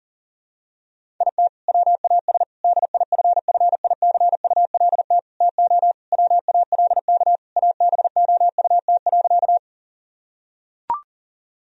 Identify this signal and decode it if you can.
Morse code, heard as a keyed tone, about 40 wpm, 700 Hz. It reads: IT WAS DIFFICULT TO WALK ABOUT.
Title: Sentences from Top 1000 Words - Review Part 1 - 40wpm